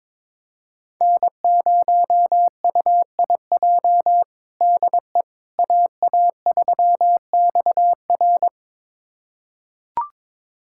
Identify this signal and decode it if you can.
Morse code, heard as a keyed tone, about 22 wpm, 700 Hz. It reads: N0UIJ DE AA3XR
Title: USA Callsign DE Callsign - 5 Characters - 22wpm